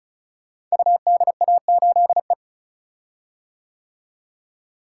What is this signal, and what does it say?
Morse code, heard as a keyed tone, about 36 wpm, 700 Hz. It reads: UDA8E
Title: Code Groups - 5 Letter:Numbers - 35wpm